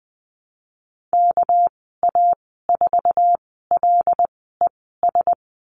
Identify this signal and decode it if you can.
Morse code, heard as a keyed tone, about 20 wpm, spaced as for 15 wpm, 700 Hz. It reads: KA4LES
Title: 219 Lesson 35g - Number 0 - Call Signs 20wpm